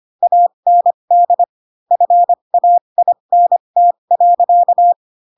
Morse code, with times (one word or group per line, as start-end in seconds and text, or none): 0.2-1.4 AND
1.9-4.9 FAINT.